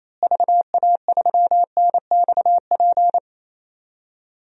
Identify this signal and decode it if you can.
Morse code, heard as a keyed tone, about 28 wpm, 700 Hz. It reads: VA3NXP
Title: DXCC-100 Call Signs - 6 Characters - Repeated 2x - 28wpm